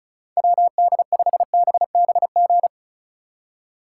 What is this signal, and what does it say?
Morse code, heard as a keyed tone, about 42 wpm, 700 Hz. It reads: WD5BBG
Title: Speed Racing - USA 2x3 Call Signs 35wpm